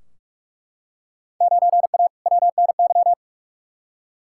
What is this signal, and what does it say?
Morse code, heard as a keyed tone, about 41 wpm, 700 Hz. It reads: 9A WNY
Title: Field Day Exchanges Encore - 45wpm